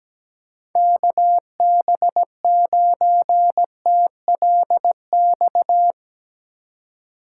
Morse code, heard as a keyed tone, about 17 wpm, 700 Hz. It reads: KB9TLX
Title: Speed Racing - USA 2x3 Call Signs 17wpm